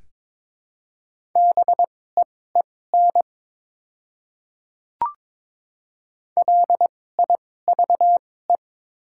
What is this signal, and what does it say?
Morse code, heard as a keyed tone, about 22 wpm, spaced as for 16 wpm, 700 Hz. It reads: BEEN LIVE
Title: 2x Character Spacing - Top 300 Words - 22wpm